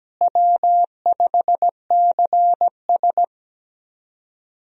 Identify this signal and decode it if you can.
Morse code, heard as a keyed tone, about 17 wpm, 700 Hz. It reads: W5CS
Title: USA Call Signs - 4 Characters - Repeated 2x - 17wpm